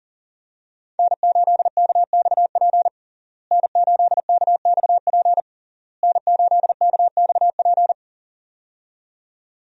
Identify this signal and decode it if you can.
Morse code, heard as a keyed tone, about 37 wpm, 700 Hz. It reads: N8KXP N8KXP N8KXP
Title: USA Call Signs - 5 Characters - Repeated 3x - 40wpm